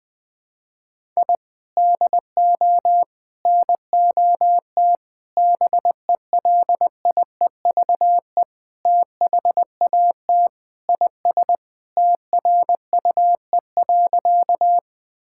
Morse code, heard as a keyed tone, about 21 wpm, 700 Hz. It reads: I DO NOT BELIEVE THAT IS TRUE.
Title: Sentences with 4-Word Phrases - Spoken Beginning - 20wpm